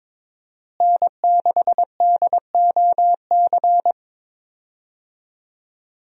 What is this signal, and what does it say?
Morse code, heard as a keyed tone, about 21 wpm, 700 Hz. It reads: N6DOC
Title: USA Call Signs - 5 Characters - Repeated 2x - 22wpm